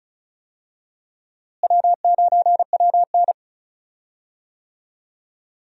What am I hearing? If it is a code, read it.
Morse code, heard as a keyed tone, about 34 wpm, 700 Hz. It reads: W9WN
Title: USA Call Signs - 4 Characters - Repeated 3x - 35wpm